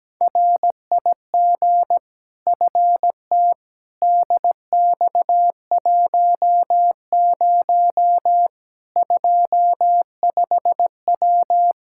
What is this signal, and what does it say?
Morse code, heard as a keyed tone, about 17 wpm, 700 Hz. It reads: RIG FTDX10 25W
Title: Practice QSO - Exchange 2 - Info - 17wpm